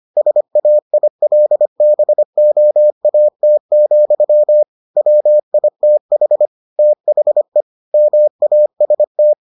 Morse code, heard as a keyed tone, about 25 wpm, 600 Hz. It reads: SAILBOAT, WITH THE MAST